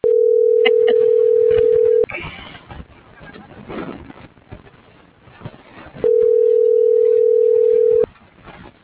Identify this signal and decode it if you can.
Ringback tone (North America), 440 + 480 Hz.